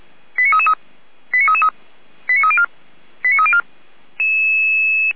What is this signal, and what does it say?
Five-tone selective call (ZVEI), heard as quick runs of five tones, tones of about 70 ms. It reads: 89333 89304 89304